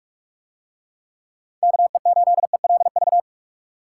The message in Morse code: KE8ELV

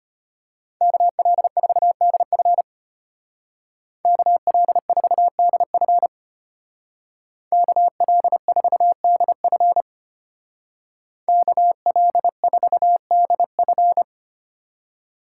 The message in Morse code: KL4DF KL4DF KL4DF KL4DF

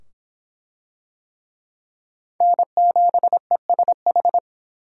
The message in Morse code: N7ESH

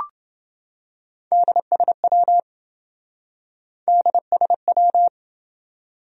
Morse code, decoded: DSW DSW

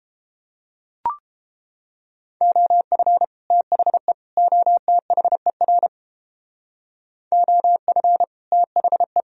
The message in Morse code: OF THE OTHER OF THE

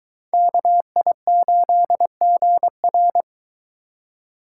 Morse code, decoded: KI8GR

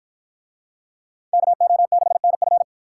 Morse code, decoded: KO6NF

